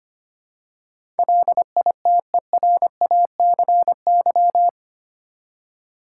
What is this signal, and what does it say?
Morse code, heard as a keyed tone, about 25 wpm, 700 Hz. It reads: LITERACY
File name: Eight Letter Words - 25wpm